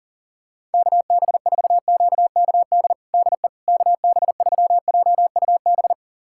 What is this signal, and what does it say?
Morse code, heard as a keyed tone, about 40 wpm, 700 Hz. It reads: KB4QKD DE KB3JUB
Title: USA Callsign DE Callsign - 6 Characters - 40wpm